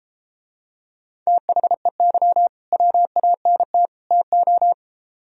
Morse code, heard as a keyed tone, about 33 wpm, 700 Hz. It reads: THEY WANT TO